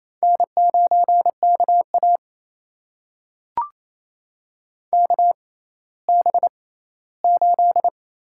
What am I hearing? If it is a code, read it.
Morse code, heard as a keyed tone, about 28 wpm, spaced as for 16 wpm, 700 Hz. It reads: N9KA KB8
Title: USA Call Signs - 6x Character Spacing - 28wpm